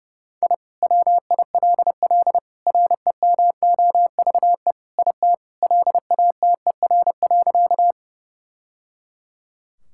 Morse code, heard as a keyed tone, about 30 wpm, 700 Hz. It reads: I WILL REMOVE IT LATER.
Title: Sentences from Top 1000 Words - Review Part 5 - 30wpm